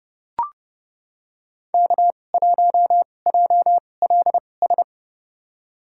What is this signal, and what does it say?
Morse code, heard as a keyed tone, about 30 wpm, spaced as for 22 wpm, 700 Hz. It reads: K1JLS
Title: USA Call Signs - 5 Characters - 2x Character Spacing - 30wpm